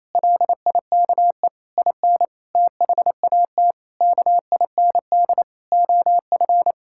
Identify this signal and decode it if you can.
Morse code, heard as a keyed tone, about 28 wpm, 700 Hz. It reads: LIKE IN THAT KIND OF
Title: Sentences from Top 1000 Words - Review Part 5 - 28wpm